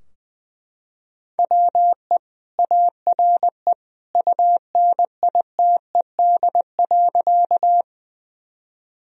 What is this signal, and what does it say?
Morse code, heard as a keyed tone, about 20 wpm, 700 Hz. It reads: WE ARE UNITED.
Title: Sentences from Top 1000 Words - Review Part 4 - 20wpm